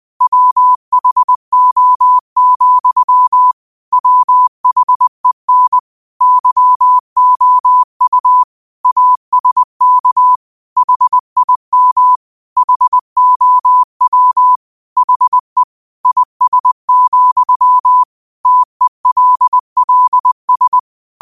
Morse code, heard as a keyed tone, about 20 wpm, 1000 Hz. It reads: WHO, WHEN YOU ASK HIM HOW HE IS, TELLS
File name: qotdfiles / 20wpm